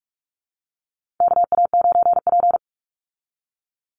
Morse code, heard as a keyed tone, about 44 wpm, 700 Hz. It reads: KA9P
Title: Call Signs Repeated 2x - 45wpm